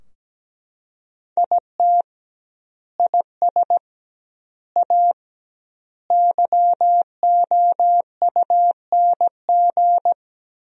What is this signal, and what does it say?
Morse code, heard as a keyed tone, about 17 wpm, 700 Hz. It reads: IT IS A YOUNG
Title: 2x Word-Spacing - Sentences from Top 400 Words - 17wpm